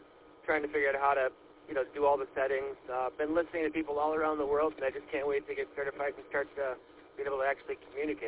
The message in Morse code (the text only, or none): none